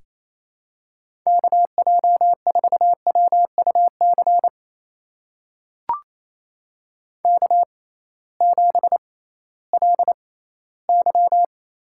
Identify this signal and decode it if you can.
Morse code, heard as a keyed tone, about 28 wpm, spaced as for 16 wpm, 700 Hz. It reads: KJ4WUC K7LY
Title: USA Call Signs - 6x Character Spacing - 28wpm